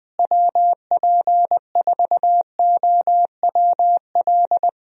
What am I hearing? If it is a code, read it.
Morse code, heard as a keyed tone, about 21 wpm, 700 Hz. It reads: WP4OWL
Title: USA Call Signs - 6 Characters - Repeated 3x - 20wpm